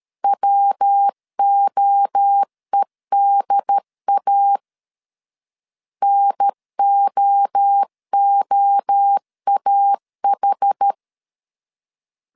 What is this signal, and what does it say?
Morse code, heard as a keyed tone, about 13 wpm, 800 Hz. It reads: WOEDA NOOAH
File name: CW / DET / 1 3 1500C S C 12wpm 2750